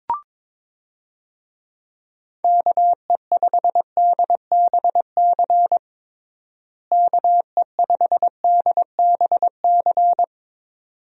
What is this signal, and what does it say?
Morse code, heard as a keyed tone, about 22 wpm, 700 Hz. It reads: KE5DBC KE5DBC
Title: USA Call Signs - 6 Characters - Repeated 2x - 22wpm